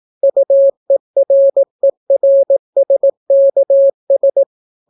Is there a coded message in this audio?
Yes: Morse code from 0.2 to 4.4 s, heard as a keyed tone, about 18 wpm, 550 Hz. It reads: UERERSKS